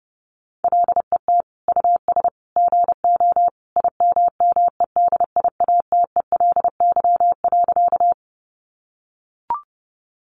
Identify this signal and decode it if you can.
Morse code, heard as a keyed tone, about 30 wpm, 700 Hz. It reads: LET US GO IMMEDIATELY.
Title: Sentences from Top 1100 Words 30wpm